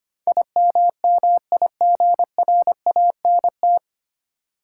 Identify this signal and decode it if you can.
Morse code, heard as a keyed tone, about 25 wpm, 700 Hz. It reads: IMMIGRANT
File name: Nine Letter Words - 25wpm